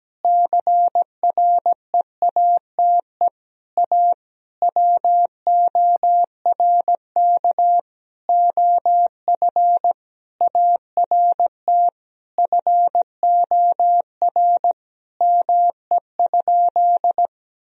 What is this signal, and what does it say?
Morse code, heard as a keyed tone, about 17 wpm, 700 Hz. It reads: CREATE A WORK OF ART FOR ME?
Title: Sentences from Top 1000 Words - Review Part 3 - 17wpm